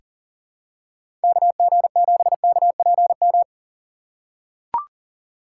Morse code, heard as a keyed tone, about 38 wpm, 700 Hz. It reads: KG7KPM
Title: USA Call Signs - 6 Characters - 4x Character Spacing - 40wpm